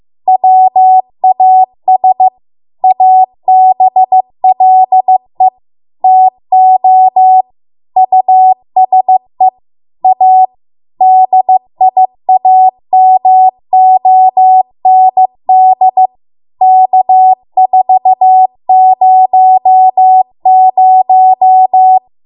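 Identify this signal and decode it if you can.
Morse code, heard as a keyed tone, about 15 wpm, 750 Hz. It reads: WAS ABLE TO USE A DIAMOND K400
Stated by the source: Morse code practice at 15 wpm from the national amateur radio association's code archive.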